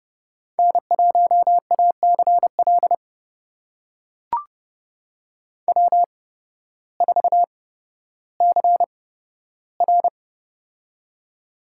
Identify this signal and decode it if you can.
Morse code, heard as a keyed tone, about 30 wpm, spaced as for 14 wpm, 700 Hz. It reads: N1ACL W4CR
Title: USA Call Signs - 8x Character Spacing - 30wpm